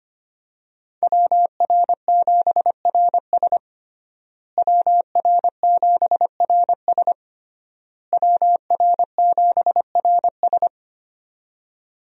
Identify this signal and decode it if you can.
Morse code, heard as a keyed tone, about 25 wpm, 700 Hz. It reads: WR7RS WR7RS WR7RS
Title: USA Call Signs - 5 Characters - Repeated 3x - 25wpm